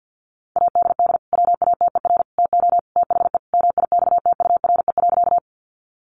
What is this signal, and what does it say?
Morse code, heard as a keyed tone, about 50 wpm, 700 Hz. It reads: ADD WATER TO THE MIXTURE.
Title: Sentences from Top 1500 Words 50wpm